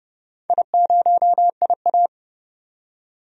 Morse code, heard as a keyed tone, about 30 wpm, 700 Hz. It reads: I0IA
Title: DXCC-100 Call Signs - 4 Characters - 30wpm